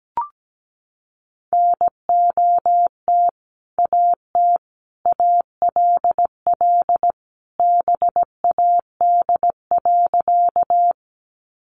Morse code, heard as a keyed tone, about 17 wpm, 700 Hz. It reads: NOT AT ALL BAD.
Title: Sentences from Top 700 Words 17wpm